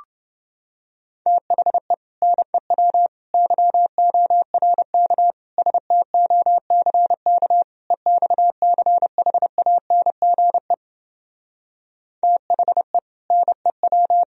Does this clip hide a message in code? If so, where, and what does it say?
Yes: Morse code from 1.3 to 14.2 s, heard as a keyed tone, about 30 wpm, 700 Hz. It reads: THE NEW YORK STOCK EXCHANGE THE NEW